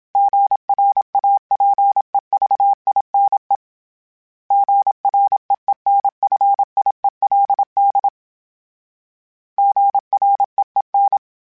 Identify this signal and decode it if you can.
Morse code, heard as a keyed tone, about 27 wpm, 800 Hz. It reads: GRAPEVINE GREENFIELD GREEN
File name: fromSMs / FromSMs mp3s / Cities@25